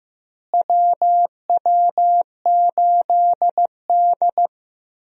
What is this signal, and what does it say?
Morse code, heard as a keyed tone, about 15 wpm, 700 Hz. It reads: WW8D
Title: USA Call Signs - 4 Characters - Repeated 2x - 15wpm